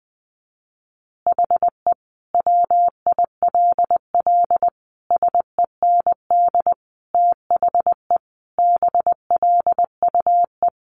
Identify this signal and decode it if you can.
Morse code, heard as a keyed tone, about 20 wpm, 700 Hz. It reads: HE WILL SEND THE BLUE
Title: Additional Easy Phrases 20wpm